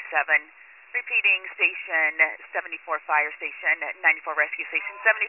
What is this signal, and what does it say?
Emergency Alert System (SAME) tones, heard as harsh buzzing bursts, with speech. It is heard from 0.2 to 4.2 s.